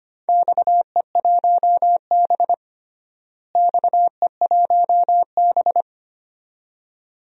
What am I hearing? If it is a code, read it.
Morse code, heard as a keyed tone, about 25 wpm, 700 Hz. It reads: XE1B XE1B
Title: DXCC-100 Call Signs - 4 Characters - Repeated 2x - 25wpm